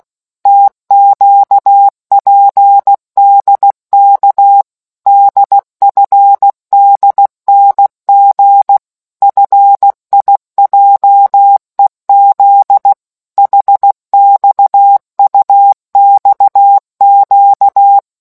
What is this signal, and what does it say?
Morse code, heard as a keyed tone, about 16 wpm, 800 Hz. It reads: TQPDK DFDNG FIJEZ HXUXQ